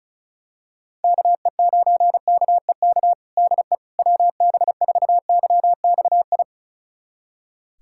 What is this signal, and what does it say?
Morse code, heard as a keyed tone, about 36 wpm, 700 Hz. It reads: KE9KEK DE WB4YXI
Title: USA Callsign DE Callsign - 6 Characters - 35wpm